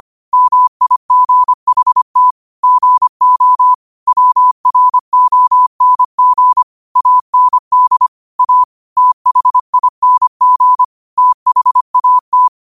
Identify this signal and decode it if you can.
Morse code, heard as a keyed tone, about 25 wpm, 1000 Hz. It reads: MIGHT GO WRONG AND A THING THAT
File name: qotdfiles / 25wpm